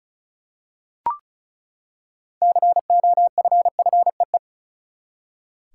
Morse code, heard as a keyed tone, about 34 wpm, 700 Hz. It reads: COFFEE
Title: Six Letter Words - 35wpm